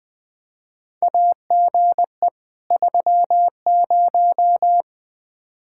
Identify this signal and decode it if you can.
Morse code, heard as a keyed tone, about 20 wpm, 700 Hz. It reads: AGE 30
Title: Practice QSO - Exchange 3 - More Info - 20wpm